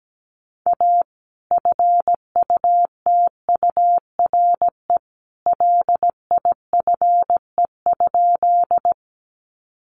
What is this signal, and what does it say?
Morse code, heard as a keyed tone, about 17 wpm, 700 Hz. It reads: A FUTURE LIFE?
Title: Sentences from Top 700 Words 17wpm